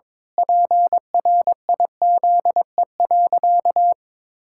Morse code, heard as a keyed tone, about 22 wpm, 700 Hz. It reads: PRIZE.
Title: Pangram Challenge v2 - 22wpm